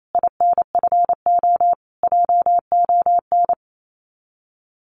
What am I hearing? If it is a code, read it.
Morse code, heard as a keyed tone, about 28 wpm, 700 Hz. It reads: INFO JON
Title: Common QSO Phrases Encore 28wpm